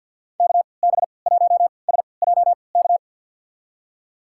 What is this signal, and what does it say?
Morse code, heard as a keyed tone, about 46 wpm, spaced as for 27 wpm, 700 Hz. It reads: KB1SJK
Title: USA Call Signs - 6 Characters - 3x Character Spacing - 50wpm